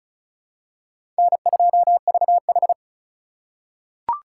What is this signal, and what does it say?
Morse code, heard as a keyed tone, about 36 wpm, 700 Hz. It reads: N2VH
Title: USA Call Signs - 4 Characters - 3x Character Spacing - 35wpm